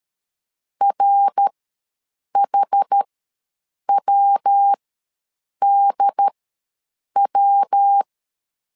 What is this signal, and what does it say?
Morse code, heard as a keyed tone, about 13 wpm, spaced as for 7 wpm, 800 Hz. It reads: RHWDW
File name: LEZIONI / 3 MWRO / 3 MWR ws 12wpm 1809